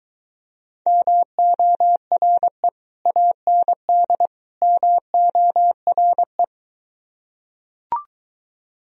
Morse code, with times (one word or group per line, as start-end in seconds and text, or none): 0.9-2.7 MORE
3.1-4.3 AND
4.6-6.4 MORE